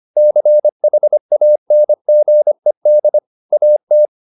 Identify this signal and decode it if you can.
Morse code, heard as a keyed tone, about 25 wpm, 600 Hz. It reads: CHANGED AT